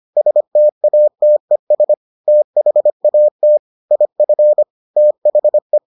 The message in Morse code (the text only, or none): STATES THAT IF THE